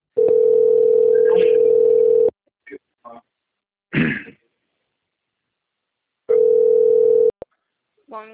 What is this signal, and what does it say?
Ringback tone (North America), 440 + 480 Hz, with speech.
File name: out- - - - - .34917